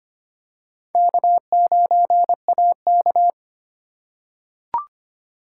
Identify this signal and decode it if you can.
Morse code, heard as a keyed tone, about 25 wpm, 700 Hz. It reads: K9AK
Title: USA Call Signs - 4 Characters - 3x Character Spacing - 25wpm